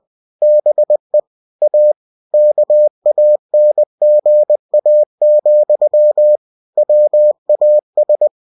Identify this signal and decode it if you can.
Morse code, heard as a keyed tone, about 20 wpm, 600 Hz. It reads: BE A KANGA, WAS
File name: cw-books / winnie-the-pooh / winnie-20wpm-page-0071